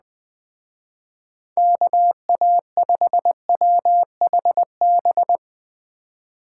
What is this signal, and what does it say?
Morse code, heard as a keyed tone, about 21 wpm, 700 Hz. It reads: KA5WHB
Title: Speed Racing - USA 2x3 Call Signs 17wpm